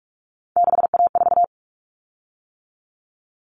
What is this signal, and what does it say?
Morse code, heard as a keyed tone, about 44 wpm, 700 Hz. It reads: 6A4
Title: Common Cut Numbers - 3-Digit 45wpm